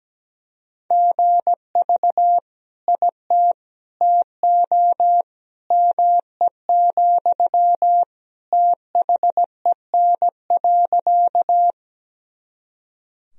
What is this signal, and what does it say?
Morse code, heard as a keyed tone, about 17 wpm, 700 Hz. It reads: GV IT TO ME, THEN.
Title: Sentences from Top 200 Words and Abbreviations 17wpm